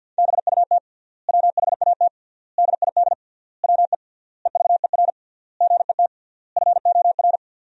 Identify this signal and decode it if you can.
Morse code, heard as a keyed tone, about 50 wpm, 700 Hz. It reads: BUT WHAT DID WE EVER GET FOR
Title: 3x Word-Spacing - Sentences from Top 400 Words - 50wpm